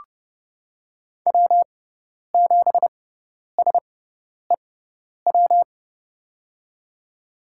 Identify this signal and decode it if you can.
Morse code, heard as a keyed tone, about 30 wpm, spaced as for 10 wpm, 700 Hz. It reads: W7SEW
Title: USA Call Signs - 6x Character Spacing - 30wpm